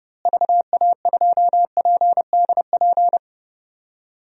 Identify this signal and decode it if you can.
Morse code, heard as a keyed tone, about 30 wpm, 700 Hz. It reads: VA2PDP
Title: DXCC-100 Call Signs - 6 Characters - Repeated 2x - 30wpm